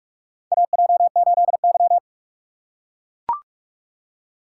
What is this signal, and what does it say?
Morse code, heard as a keyed tone, about 41 wpm, 700 Hz. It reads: AJ8Y
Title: USA Call Signs - 4 Characters - 3x Character Spacing - 45wpm